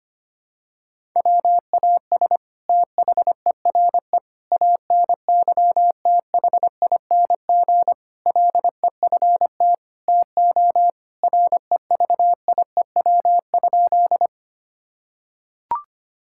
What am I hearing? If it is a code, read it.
Morse code, heard as a keyed tone, about 25 wpm, 700 Hz. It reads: WAS THERE ANYTHING LEFT TO REVIEW?
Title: Sentences from Top 1000 Words - Review Part 6 - 25wpm